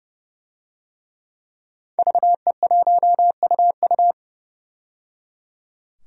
Morse code, heard as a keyed tone, about 30 wpm, 700 Hz. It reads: VE1UU
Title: DXCC-100 Call Signs - 5 Characters - 30wpm